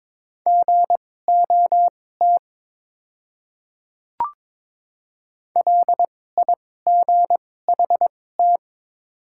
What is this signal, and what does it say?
Morse code, heard as a keyed tone, about 22 wpm, spaced as for 16 wpm, 690 Hz. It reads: GOT LIGHT
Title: 2x Character Spacing - Top 300 Words - 22wpm